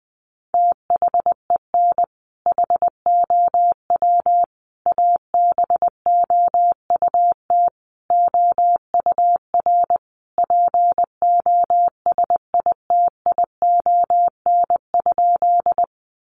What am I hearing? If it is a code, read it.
Morse code, heard as a keyed tone, about 20 wpm, 700 Hz. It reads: THEN HOW ABOUT OUR POSITION?
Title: Sentences from Top 900 Words 20wpm